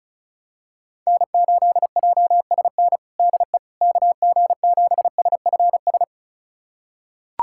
Morse code, heard as a keyed tone, about 36 wpm, 700 Hz. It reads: N8JSN DE KG7SFS
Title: USA Callsign DE Callsign 35wpm